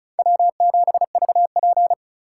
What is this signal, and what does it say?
Morse code, heard as a keyed tone, about 36 wpm, 700 Hz. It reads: W7VP
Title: K1USN SST QSO Practice - 35wpm